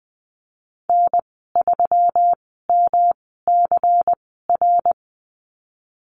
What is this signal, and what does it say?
Morse code, heard as a keyed tone, about 20 wpm, spaced as for 14 wpm, 700 Hz. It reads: N3MCR
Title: 140 Lesson 23g - Letter f - Call Signs 20wpm